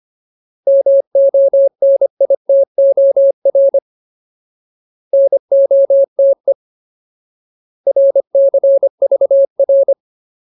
Morse code, heard as a keyed tone, about 25 wpm, 550 Hz. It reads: MONITOR NOTE RCVR
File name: Ultimate Word List / 25WPM Ultimate